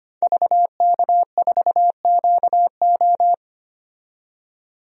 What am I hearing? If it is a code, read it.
Morse code, heard as a keyed tone, about 25 wpm, 700 Hz. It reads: VK4QO
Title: DXCC-100 Call Signs - 5 Characters - 25wpm